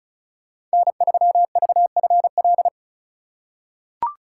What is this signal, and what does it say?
Morse code, heard as a keyed tone, about 36 wpm, 700 Hz. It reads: N3VFL
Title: USA Call Signs - 5 Characters - 2x Character Spacing - 35wpm